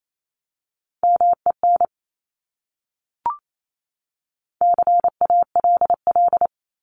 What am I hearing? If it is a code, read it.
Morse code, heard as a keyed tone, about 28 wpm, 700 Hz. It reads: MEN CALL
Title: Top 200 Words 28wpm